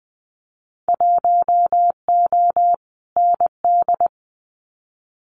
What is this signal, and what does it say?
Morse code, heard as a keyed tone, about 20 wpm, 700 Hz. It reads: 1O ND
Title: Winter Field Day Exchanges 20wpm